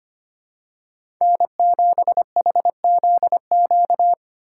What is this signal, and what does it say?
Morse code, heard as a keyed tone, about 25 wpm, 700 Hz. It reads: N7HZQ
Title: USA Call Signs - 5 Characters - 2x Character Spacing - 25wpm